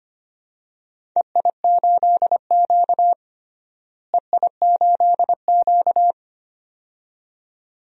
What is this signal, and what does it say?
Morse code, heard as a keyed tone, about 25 wpm, 700 Hz. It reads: EI8Q EI8Q
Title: DXCC-100 Call Signs - 4 Characters - Repeated 2x - 25wpm